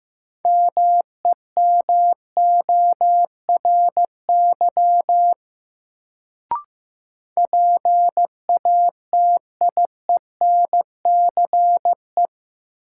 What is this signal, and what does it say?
Morse code, heard as a keyed tone, about 15 wpm, 700 Hz. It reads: MEMORY PATIENCE
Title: Sentences from Top 1300 Words - New Word Drill - 15wpm